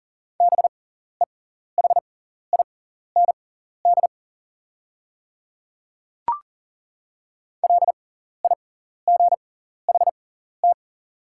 Morse code, heard as a keyed tone, about 40 wpm, spaced as for 14 wpm, 690 Hz. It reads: BEHIND LIGHT
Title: 6x Character Spacing - Top 500 Words - 40wpm